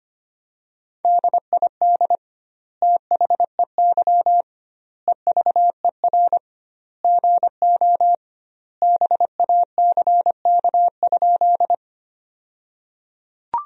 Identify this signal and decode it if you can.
Morse code, heard as a keyed tone, about 25 wpm, 700 Hz. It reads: DID THEY EVER GO BACK?
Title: 2x Word-Spacing - Sentences from Top 400 Words - 25wpm